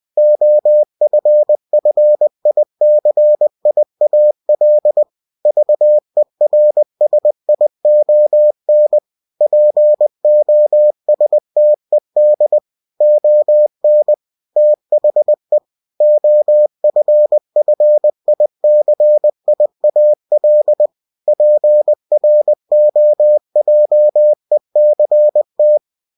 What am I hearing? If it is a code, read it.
Morse code, heard as a keyed tone, about 20 wpm, 600 Hz. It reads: OFFICIAL VERSION POSTED ON THE OFFICIAL PROJECT
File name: cw-books / winnie-the-pooh / winnie-20wpm-page-0116